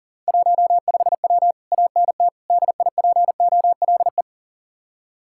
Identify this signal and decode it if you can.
Morse code, heard as a keyed tone, about 48 wpm, 700 Hz. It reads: 15W ANT DIPOLE
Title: Speed-Racing Common QSO Phrases Encore - 40wpm